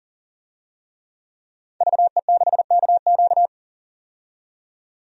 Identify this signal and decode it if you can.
Morse code, heard as a keyed tone, about 40 wpm, 700 Hz. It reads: VE6KQ
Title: DXCC-100 Call Signs - 5 Characters - 40wpm